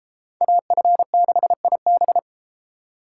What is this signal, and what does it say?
Morse code, heard as a keyed tone, about 30 wpm, 700 Hz. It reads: AF6IB